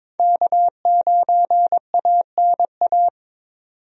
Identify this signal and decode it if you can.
Morse code, heard as a keyed tone, about 22 wpm, 700 Hz. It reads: K9ANA
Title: USA Call Signs - 5 Characters - 22wpm